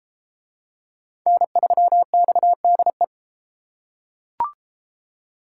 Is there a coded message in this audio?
Yes: Morse code from 1.3 to 3.0 s, heard as a keyed tone, about 32 wpm, 700 Hz. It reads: N3XDE